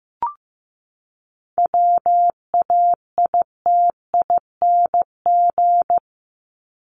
Morse code, heard as a keyed tone, about 15 wpm, 700 Hz. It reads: WAITING
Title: ING Suffix - Top 100 Words with Base Words 15wpm